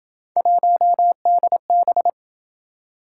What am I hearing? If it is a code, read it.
Morse code, heard as a keyed tone, about 27 wpm, 700 Hz. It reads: 1DB